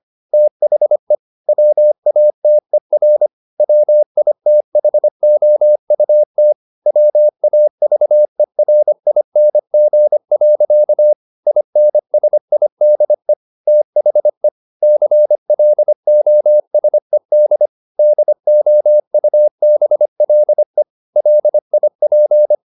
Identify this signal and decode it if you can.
Morse code, heard as a keyed tone, about 25 wpm, 600 Hz. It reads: THE WATER WITHOUT WAVERING. INSIDE THE CLOSED DOUBLE LIP